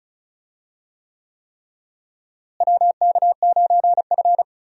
Morse code, heard as a keyed tone, about 36 wpm, 700 Hz. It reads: WK9F
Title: USA Call Signs - 4 Characters - Repeated 3x - 35wpm